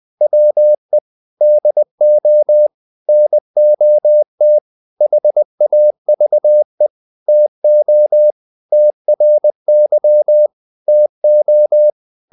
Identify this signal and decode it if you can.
Morse code, heard as a keyed tone, about 20 wpm, 600 Hz. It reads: WE DO NOT HAVE TO TRY TO
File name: the old man and the sea part 2 20wpm